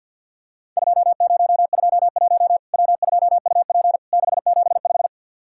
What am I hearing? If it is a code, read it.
Morse code, heard as a keyed tone, about 50 wpm, 700 Hz. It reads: 2021 W2UP 675
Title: Worked All Europe DX Contest - QTC Exchange with Repeats - 50wpm